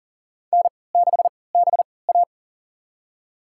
Morse code, heard as a keyed tone, about 40 wpm, spaced as for 23 wpm, 700 Hz. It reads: N6BA
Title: USA Call Signs - 4 Characters - 3x Character Spacing - 40wpm